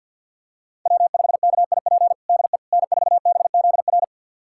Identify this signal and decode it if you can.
Morse code, heard as a keyed tone, about 49 wpm, 700 Hz. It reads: W5KIP DE N4BZR
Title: USA Callsign DE Callsign - 5 Characters - 50wpm